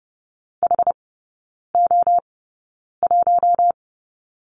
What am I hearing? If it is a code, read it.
Morse code, heard as a keyed tone, about 30 wpm, spaced as for 9 wpm, 700 Hz. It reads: HO1
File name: Lesson 18e - Letter m - Three Characters 30wpm